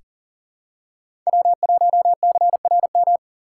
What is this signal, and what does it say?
Morse code, heard as a keyed tone, about 40 wpm, 700 Hz. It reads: W1CRM
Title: USA Call Signs - 5 Characters - 2x Character Spacing - 40wpm